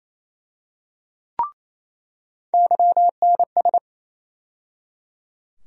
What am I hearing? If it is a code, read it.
Morse code, heard as a keyed tone, about 28 wpm, 700 Hz. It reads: YNS
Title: USA Call Sign Postfixes - 3 Letters 28wpm